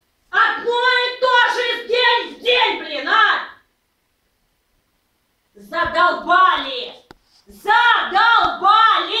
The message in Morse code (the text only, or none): none